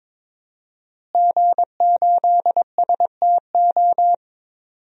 Morse code, heard as a keyed tone, about 22 wpm, 700 Hz. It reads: G8STO
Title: DXCC-100 Call Signs - 5 Characters - Repeated 3x - 22wpm